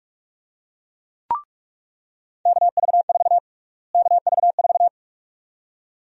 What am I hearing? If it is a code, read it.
Morse code, heard as a keyed tone, about 46 wpm, 700 Hz. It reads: KV4 KV4
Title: USA Call Sign Prefixes - Repeated 2x - 45wpm